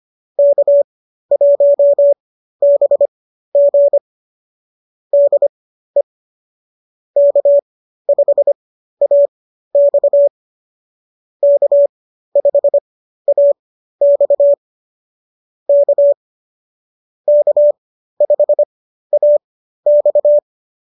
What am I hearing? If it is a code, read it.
Morse code, heard as a keyed tone, about 25 wpm, spaced as for 13 wpm, 590 Hz. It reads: K1BG DE K5AX K5AX K K5AX